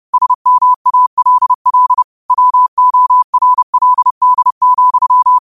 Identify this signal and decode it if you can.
Morse code, heard as a keyed tone, about 30 wpm, 1000 Hz. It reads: SMALL WORLD,
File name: qotdfiles / 30wpm